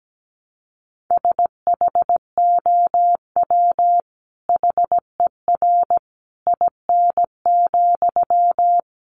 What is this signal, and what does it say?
Morse code, heard as a keyed tone, about 17 wpm, 700 Hz. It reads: SHOW HER IN,
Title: Sentences from Top 1100 Words 17wpm